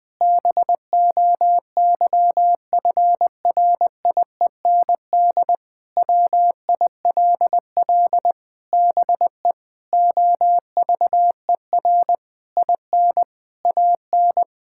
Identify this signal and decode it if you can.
Morse code, heard as a keyed tone, about 20 wpm, 700 Hz. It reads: BOYFRIEND WILL BE OVER IN AN
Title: Sentences from Top 1000 Words - Review Part 5 - 20wpm